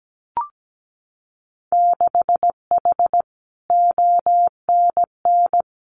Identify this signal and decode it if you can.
Morse code, heard as a keyed tone, about 17 wpm, 700 Hz. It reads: 6H ONN
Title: Winter Field Day Exchanges Encore 17wpm